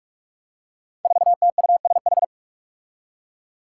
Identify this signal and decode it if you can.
Morse code, heard as a keyed tone, about 46 wpm, 700 Hz. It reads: 4TUSH
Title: Five Challenging Letter-Number Combinations 45wpm